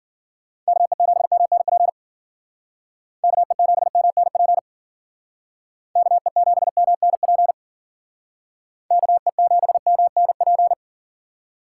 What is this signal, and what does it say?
Morse code, heard as a keyed tone, about 49 wpm, 700 Hz. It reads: KE7MNP KE7MNP KE7MNP KE7MNP
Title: Speed Racing - USA Call Signs 40wpm